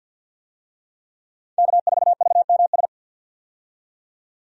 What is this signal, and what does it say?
Morse code, heard as a keyed tone, about 50 wpm, 700 Hz. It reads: K4VMS
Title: USA Call Signs - 5 Characters - 50wpm